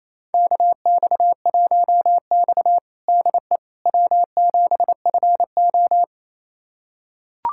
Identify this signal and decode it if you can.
Morse code, heard as a keyed tone, about 27 wpm, 700 Hz. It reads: KX1X DE W7FO
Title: USA Callsign DE Callsign - 4 Characters - 28wpm